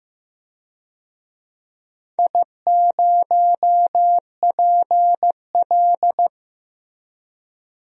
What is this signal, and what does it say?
Morse code, heard as a keyed tone, about 15 wpm, 700 Hz. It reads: I0PL
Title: DXCC-100 Call Signs - 4 Characters - 15wpm